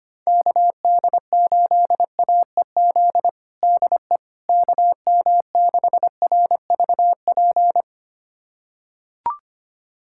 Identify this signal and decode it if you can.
Morse code, heard as a keyed tone, about 25 wpm, 700 Hz. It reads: KD8AEZ DE KM6RVP
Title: USA Callsign DE Callsign - 6 Characters - 25wpm